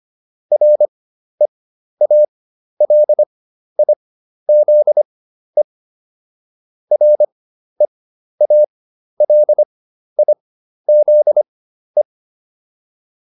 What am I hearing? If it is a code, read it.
Morse code, heard as a keyed tone, about 25 wpm, spaced as for 12 wpm, 600 Hz. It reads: REALIZE REALIZE